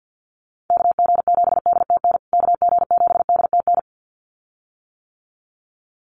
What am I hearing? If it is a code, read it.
Morse code, heard as a keyed tone, about 50 wpm, 700 Hz. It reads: KG7DTN KG7DTN
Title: USA Call Signs Repeated 2x 50wpm